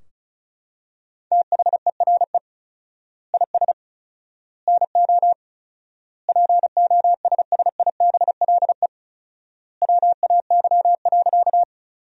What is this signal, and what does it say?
Morse code, heard as a keyed tone, about 34 wpm, 700 Hz. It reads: THERE IS NO POSSIBLE WAY.
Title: 4x Word-Spacing - Sentences from Top 500 Words - 35wpm